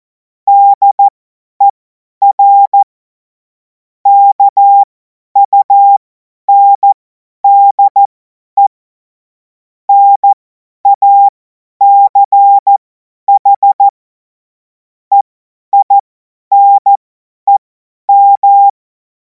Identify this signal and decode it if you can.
Morse code, heard as a keyed tone, about 14 wpm, spaced as for 10 wpm, 800 Hz. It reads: DER KUNDE NACH EINEM
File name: Morsekurs / Witze / Witz 03. Februar 2026 50 F70